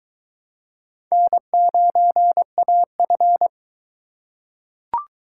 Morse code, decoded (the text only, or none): N9AF